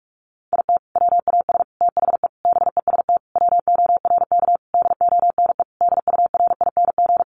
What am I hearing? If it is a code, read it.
Morse code, heard as a keyed tone, about 44 wpm, 700 Hz. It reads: IT WAS THE BEST WORK DONE DURING
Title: Sentences from Top 1300 Words 45wpm